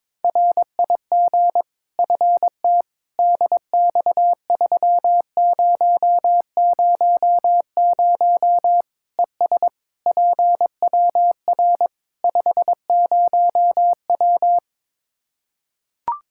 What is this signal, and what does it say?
Morse code, heard as a keyed tone, about 22 wpm, 700 Hz. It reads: RIG FT DX3000 ES PWR 50W
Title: Practice QSO - Exchange 2 - Info - 22wpm